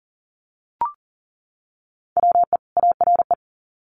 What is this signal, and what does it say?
Morse code, heard as a keyed tone, about 40 wpm, 700 Hz. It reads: WE ARE
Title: Sets of 2 Words 40wpm